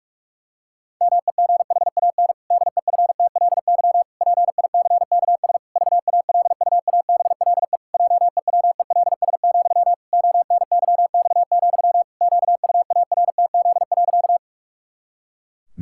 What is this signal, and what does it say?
Morse code, heard as a keyed tone, about 46 wpm, 700 Hz. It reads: MEGHAN DEFTLY PICKS VALUABLE JEWELS, ONYX, QUARTZ.
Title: Pangram Challenge v2 - 45wpm